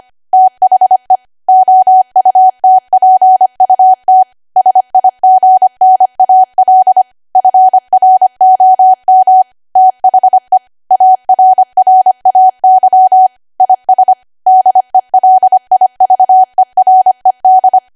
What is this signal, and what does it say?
Morse code, heard as a keyed tone, about 25 wpm, 750 Hz. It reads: THE OUTPUT SIGNAL FROM THE ARRAY IS DELIVERED
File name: Archive / 25 WPM / 25WPM